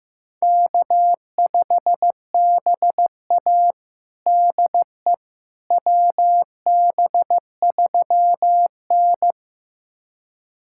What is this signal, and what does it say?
Morse code, heard as a keyed tone, about 15 wpm, 700 Hz. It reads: K5BA DE WB3N
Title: USA Callsign DE Callsign - 4 Characters - 15wpm